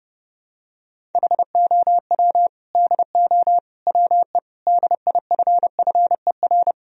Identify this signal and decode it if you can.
Morse code, heard as a keyed tone, about 30 wpm, 700 Hz. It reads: HOW DO WE DIFFER
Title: Sentences from Top 1000 Words - Review Part 1 - 30wpm